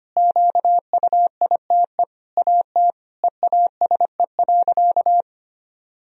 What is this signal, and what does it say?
Morse code, heard as a keyed tone, about 25 wpm, 700 Hz. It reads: QUITE AT EASE.
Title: Sentences from Top 1000 Words - Review Part 1 - 25wpm